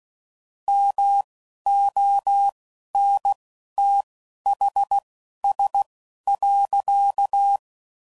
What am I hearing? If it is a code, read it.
Morse code, heard as a keyed tone, about 16 wpm, spaced as for 11 wpm, 800 Hz. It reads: MONTHS.
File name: CW Code Groups @15WPM / QSOs / WD0BGP Spaced